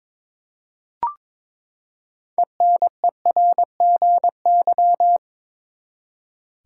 Morse code, heard as a keyed tone, about 22 wpm, 700 Hz. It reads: ENERGY
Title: Six Letter Words - 22wpm